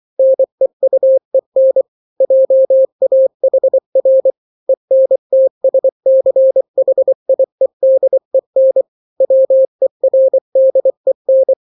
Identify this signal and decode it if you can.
Morse code, heard as a keyed tone, about 24 wpm, 530 Hz. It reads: NEUEN JAHR ENTSCHIEDEN WERDEN
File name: daily / 23wpm